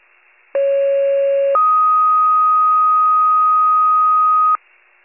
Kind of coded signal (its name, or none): two-tone page